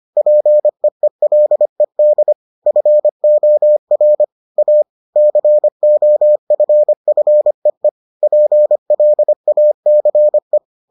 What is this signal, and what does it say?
Morse code, heard as a keyed tone, about 25 wpm, 600 Hz. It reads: PEELED FOR A COFFEE PLACE